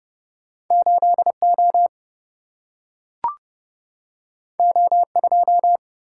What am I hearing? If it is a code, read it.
Morse code, heard as a keyed tone, about 30 wpm, 700 Hz. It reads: 8 O O 2
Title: Single-Letter Single-Number 30wpm